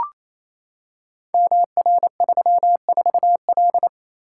Morse code, heard as a keyed tone, about 29 wpm, 700 Hz. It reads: MR34L
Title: Code Groups - 5 Letter:Numbers - 28wpm